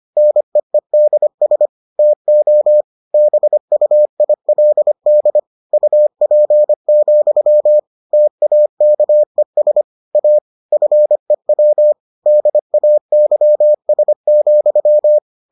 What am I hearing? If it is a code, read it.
Morse code, heard as a keyed tone, about 25 wpm, 600 Hz. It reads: NEEDS TO BUILD UP, TAKES A FEW DAYS,